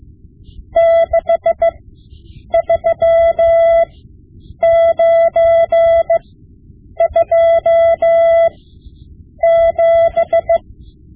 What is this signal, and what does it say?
Morse code, heard as a keyed tone, about 15 wpm, spaced as for 8 wpm, 670 Hz. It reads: 63927
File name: Morse Lessons / Morse Lessons / Tape-5B